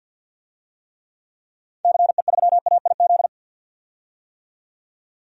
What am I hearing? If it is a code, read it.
Morse code, heard as a keyed tone, about 52 wpm, 700 Hz. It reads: CE3AIZ
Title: DXCC-100 Call Signs - 6 Characters - 50wpm